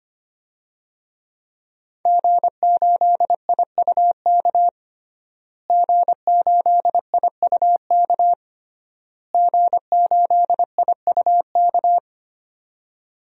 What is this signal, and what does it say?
Morse code, heard as a keyed tone, about 25 wpm, 700 Hz. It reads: G8IUK G8IUK G8IUK
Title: DXCC-100 Call Signs - 5 Characters - Repeated 3x - 25wpm